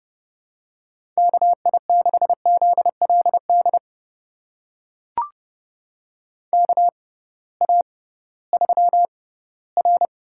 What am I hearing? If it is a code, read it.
Morse code, heard as a keyed tone, about 32 wpm, spaced as for 18 wpm, 700 Hz. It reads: KI6ZLD KA3R
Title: USA Call Signs - 6 Characters - 6x Character Spacing - 30wpm